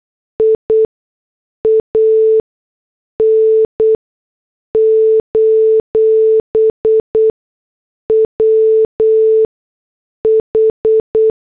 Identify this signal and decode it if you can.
Morse code, heard as a keyed tone, about 8 wpm, 440 Hz. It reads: I A N : W H
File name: morse / headlines